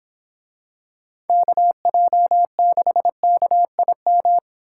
Morse code, heard as a keyed tone, about 25 wpm, 700 Hz. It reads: KJ6KIM